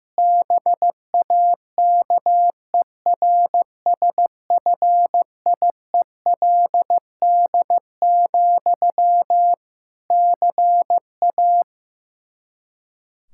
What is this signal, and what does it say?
Morse code, heard as a keyed tone, about 15 wpm, 700 Hz. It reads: BAKERSFIELD, CA
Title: K1USN SST QSO Practice 15wpm